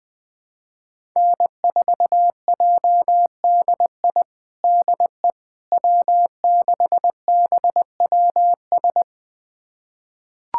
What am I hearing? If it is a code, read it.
Morse code, heard as a keyed tone, about 21 wpm, 700 Hz. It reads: N4JDI DE W6BWS
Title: USA Callsign DE Callsign - 5 Characters - 20wpm